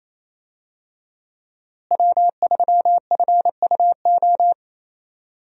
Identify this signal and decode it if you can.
Morse code, heard as a keyed tone, about 28 wpm, 700 Hz. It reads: W3FUO
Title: USA Call Signs - 5 Characters - Repeated 3x - 28wpm